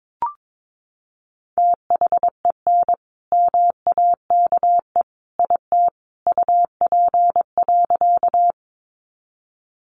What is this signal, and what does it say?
Morse code, heard as a keyed tone, about 22 wpm, 700 Hz. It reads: THEN MAKE IT UP.
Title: Sentences from Top 100 Words 22wpm